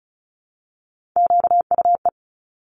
Morse code, heard as a keyed tone, about 36 wpm, 700 Hz. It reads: QUE
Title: Top Three Letter Combinations in Words 35wpm